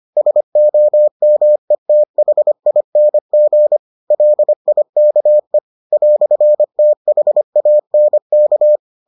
Morse code, heard as a keyed tone, about 25 wpm, 600 Hz. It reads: SOMETHING LIKE "THANK